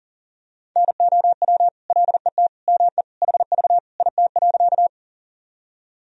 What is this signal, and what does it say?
Morse code, heard as a keyed tone, about 37 wpm, 700 Hz. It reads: NOW LET ME HV IT.
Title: Sentences from Top 300 Words and Abbreviations 40wpm